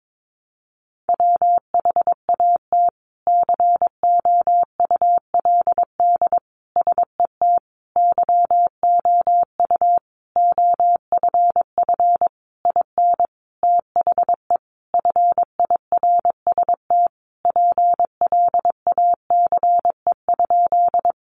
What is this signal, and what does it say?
Morse code, heard as a keyed tone, about 22 wpm, 700 Hz. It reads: WHAT COULD SET YOU OFF IN THE FIRST PLACE?
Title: Sentences from Top 200 Words 22wpm